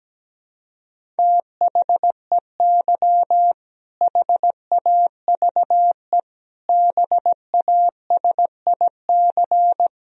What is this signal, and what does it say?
Morse code, heard as a keyed tone, about 17 wpm, 700 Hz. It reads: THEY HAVE BASIC
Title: Sentences from Top 1000 Words - Review Part 3 - 17wpm